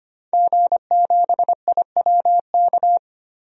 Morse code, heard as a keyed tone, about 25 wpm, 700 Hz. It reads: G7IWK
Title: DXCC-100 Call Signs - 5 Characters - 25wpm